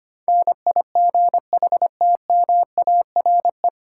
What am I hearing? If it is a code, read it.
Morse code, heard as a keyed tone, about 25 wpm, 700 Hz. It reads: NIGHTMARE
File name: Nine Letter Words - 25wpm